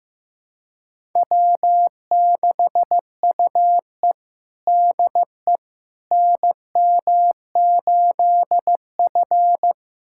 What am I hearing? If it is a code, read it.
Morse code, heard as a keyed tone, about 15 wpm, 700 Hz. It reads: W6UE DE NM8F
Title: USA Callsign DE Callsign - 4 Characters - 15wpm